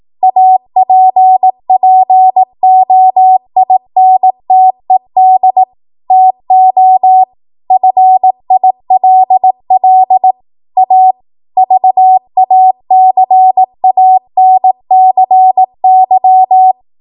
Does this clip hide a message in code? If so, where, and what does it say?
Yes: Morse code from 0.2 to 16.7 s, heard as a keyed tone, about 18 wpm, 750 Hz. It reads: APPOINTED TO FILL A VACANCY